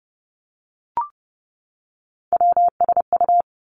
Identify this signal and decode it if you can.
Morse code, heard as a keyed tone, about 31 wpm, 700 Hz. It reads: WSU
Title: Three Challenging Letter-Number Combinations 30wpm